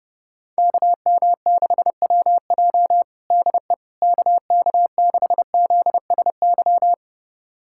Morse code, heard as a keyed tone, about 30 wpm, 700 Hz. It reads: KM6WJ DE KK6ZSY
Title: USA Callsign DE Callsign 30wpm